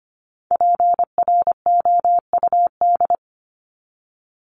Morse code, heard as a keyed tone, about 25 wpm, 700 Hz. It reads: PROUD
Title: Top 1501-1750 Words 25wpm